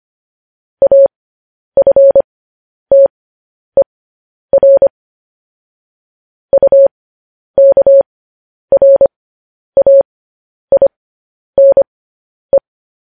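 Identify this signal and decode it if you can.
Morse code, heard as a keyed tone, about 25 wpm, spaced as for 10 wpm, 550 Hz. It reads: AFTER UKRAINE
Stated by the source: News headlines sent in Morse at 10 wpm.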